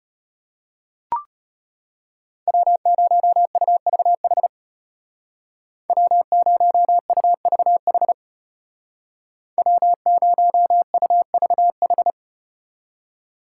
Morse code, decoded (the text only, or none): W0UVH W0UVH W0UVH